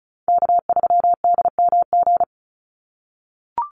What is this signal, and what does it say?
Morse code, heard as a keyed tone, about 34 wpm, 700 Hz. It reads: K3DMG
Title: Call Signs - 35wpm